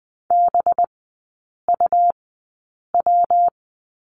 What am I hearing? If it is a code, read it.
Morse code, heard as a keyed tone, about 20 wpm, spaced as for 8 wpm, 700 Hz. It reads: BUW
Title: 184 Lesson 30e - Letter b - Three Characters 20wpm